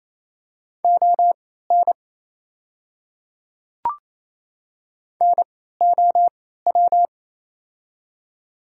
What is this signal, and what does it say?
Morse code, heard as a keyed tone, about 28 wpm, spaced as for 16 wpm, 700 Hz. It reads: ON NOW
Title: 3x Character Spacing - Top 100 Words - 28wpm